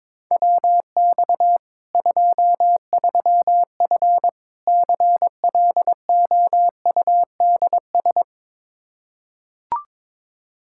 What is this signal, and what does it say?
Morse code, heard as a keyed tone, about 22 wpm, 700 Hz. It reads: WX 23F CLOUDS
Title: Practice QSO - Exchange 2 - Info - 22wpm